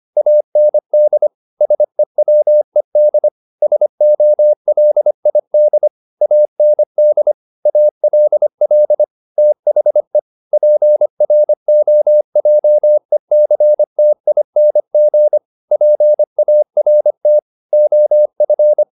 Morse code, heard as a keyed tone, about 25 wpm, 600 Hz. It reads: AND SEWED SOLID AND ALL THE PROJECTING PART OF
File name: the old man and the sea part 1 25wpm